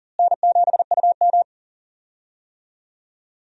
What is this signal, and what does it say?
Morse code, heard as a keyed tone, about 40 wpm, 700 Hz. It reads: N7UM
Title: USA Call Signs - 4 Characters - Repeated 3x - 40wpm